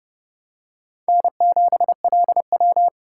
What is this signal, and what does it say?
Morse code, heard as a keyed tone, about 27 wpm, 700 Hz. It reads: N7LW